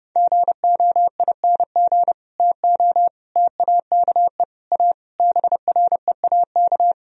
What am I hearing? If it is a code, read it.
Morse code, heard as a keyed tone, about 30 wpm, 700 Hz. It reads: GOING TO TAKE A BREAK